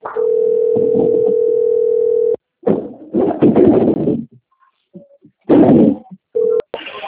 Ringback tone (North America), 440 + 480 Hz.
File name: rg-1600-unknown- - - .31454